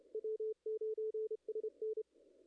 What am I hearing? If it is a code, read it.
Morse code, heard as a keyed tone, about 25 wpm, 440 Hz, with radio noise. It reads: W9SN
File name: contests / ss-cw-k5tr-op-n5rz -0000